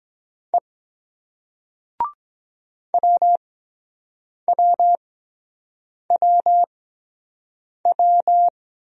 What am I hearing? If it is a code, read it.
Morse code, heard as a keyed tone, about 21 wpm, 700 Hz. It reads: E W W W W